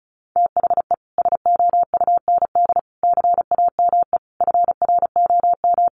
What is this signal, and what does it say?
Morse code, heard as a keyed tone, about 34 wpm, 700 Hz. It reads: THE SOUND CAME FROM
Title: Sentences from Top 1400 Words 35wpm